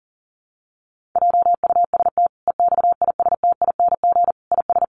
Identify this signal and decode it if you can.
Morse code, heard as a keyed tone, about 40 wpm, 700 Hz. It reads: JUST EXISTING IS
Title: Sentences from Top 1500 Words 40wpm